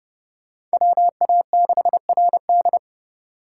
Morse code, heard as a keyed tone, about 32 wpm, 700 Hz. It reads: WA6RD